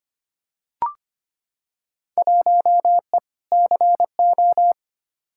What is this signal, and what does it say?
Morse code, heard as a keyed tone, about 25 wpm, 700 Hz. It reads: 1E CO
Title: Field Day Exchanges - 25wpm